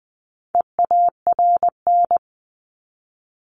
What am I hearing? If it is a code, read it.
Morse code, heard as a keyed tone, about 20 wpm, 700 Hz. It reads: EARN
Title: Top 1251-1500 Words 20wpm